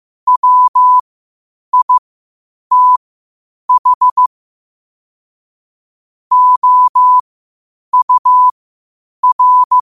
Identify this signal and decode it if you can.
Morse code, heard as a keyed tone, about 15 wpm, spaced as for 8 wpm, 1000 Hz. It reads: WITH OUR
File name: qotdfiles / 8wpm